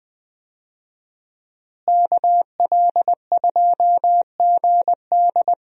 Morse code, heard as a keyed tone, about 20 wpm, 700 Hz. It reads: KL2GD
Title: USA Call Signs - 5 Characters - Repeated 2x - 20wpm